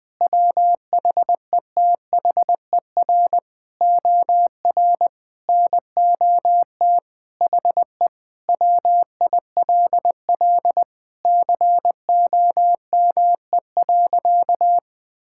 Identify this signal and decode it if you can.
Morse code, heard as a keyed tone, about 21 wpm, 700 Hz. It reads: WHETHER OR NOT HE WILL COME.
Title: Sentences with 3-Word Phrases - 20wpm